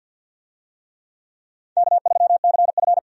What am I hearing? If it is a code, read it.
Morse code, heard as a keyed tone, about 50 wpm, 700 Hz. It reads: K3CF
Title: USA Call Signs - 4 Characters - 50wpm